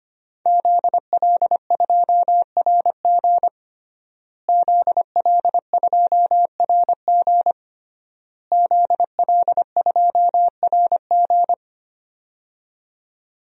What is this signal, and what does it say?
Morse code, heard as a keyed tone, about 25 wpm, 700 Hz. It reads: ZL2RG ZL2RG ZL2RG
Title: DXCC-100 Call Signs - 5 Characters - Repeated 3x - 25wpm